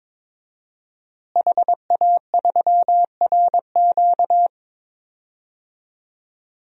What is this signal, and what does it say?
Morse code, heard as a keyed tone, about 21 wpm, 700 Hz. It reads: HA3RQ
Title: DXCC-100 Call Signs - 5 Characters - Repeated 2x - 22wpm